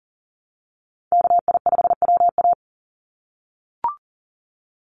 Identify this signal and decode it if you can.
Morse code, heard as a keyed tone, about 39 wpm, 700 Hz. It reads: KI5WA
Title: NA SOTA Chaser Callsigns 40wpm